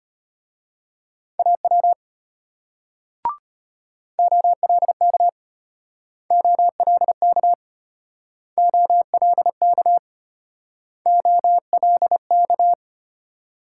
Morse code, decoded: AW OLK OLK OLK OLK

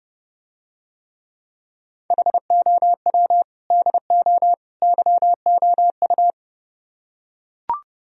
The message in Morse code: HOW DO YOU

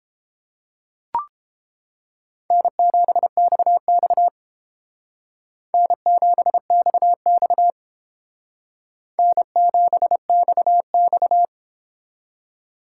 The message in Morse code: N7XX N7XX N7XX

